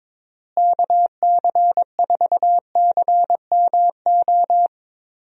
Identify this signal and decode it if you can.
Morse code, heard as a keyed tone, about 21 wpm, 700 Hz. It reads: KC4CMO
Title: USA Call Signs - 6 Characters - Repeated 3x - 22wpm